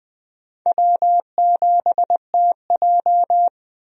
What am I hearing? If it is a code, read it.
Morse code, heard as a keyed tone, about 20 wpm, 700 Hz. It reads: W7TJ
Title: USA Call Signs - 4 Characters - 20wpm